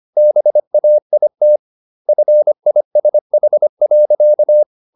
Morse code, heard as a keyed tone, about 25 wpm, 600 Hz. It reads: BAIT FISH.
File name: the old man and the sea part 1 25wpm